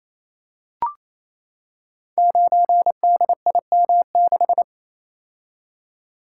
Morse code, heard as a keyed tone, about 28 wpm, 700 Hz. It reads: 9DIM6
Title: Code Groups - 5 Letter:Numbers - 28wpm